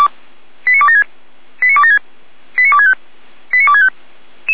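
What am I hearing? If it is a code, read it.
Five-tone selective call (ZVEI), heard as quick runs of five tones, tones of about 70 ms. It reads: 89276 89276 89265 89265
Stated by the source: ZVEI five-tone call received on an SDR.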